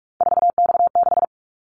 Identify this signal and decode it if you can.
Morse code, heard as a keyed tone, about 44 wpm, 700 Hz. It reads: 4X6
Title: Callsign Prefixes for Top 50 DXCC - 45wpm